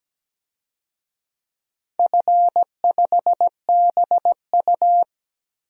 Morse code, heard as a keyed tone, about 17 wpm, 700 Hz. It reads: F5BU
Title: DXCC-100 Call Signs - 4 Characters - Repeated 2x - 17wpm